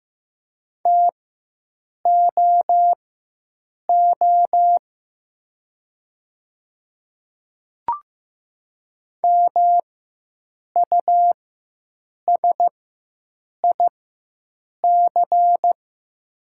Morse code, heard as a keyed tone, about 15 wpm, spaced as for 7 wpm, 700 Hz. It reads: TOO MUSIC
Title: 4x Character Spacing - Top 500 Words - 15wpm